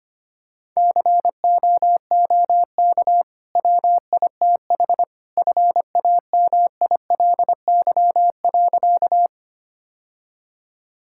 Morse code, heard as a keyed tone, about 25 wpm, 700 Hz. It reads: COOK WITH FAMILY.
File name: Easy Sentences with Hints - 25wpm